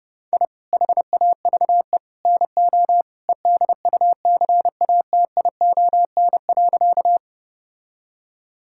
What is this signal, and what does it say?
Morse code, heard as a keyed tone, about 30 wpm, 700 Hz. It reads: I HAVE NO EDUCATION.
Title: Sentences from Top 1000 Words - Review Part 6 - 30wpm